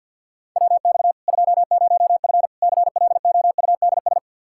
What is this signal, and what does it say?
Morse code, heard as a keyed tone, about 52 wpm, 700 Hz. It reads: WX 20F CLOUDS
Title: Practice QSO - Exchange 2 - Info - 50wpm